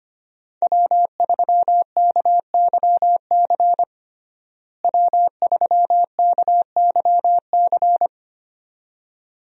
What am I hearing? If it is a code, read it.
Morse code, heard as a keyed tone, about 25 wpm, 700 Hz. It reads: W3KYC W3KYC
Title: USA Call Signs - 5 Characters - Repeated 2x - 25wpm